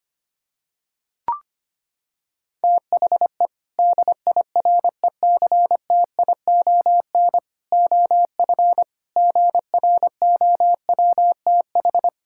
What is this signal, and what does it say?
Morse code, heard as a keyed tone, about 25 wpm, 700 Hz. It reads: THE DIRECTION OF GROWTH
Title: Sentences from Top 1000 Words - Review Part 5 - 25wpm